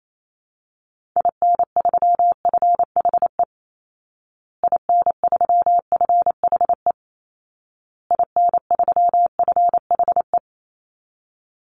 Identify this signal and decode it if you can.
Morse code, heard as a keyed tone, about 28 wpm, 700 Hz. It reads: IN3FHE IN3FHE IN3FHE
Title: Call Signs Repeated 2x - 28wpm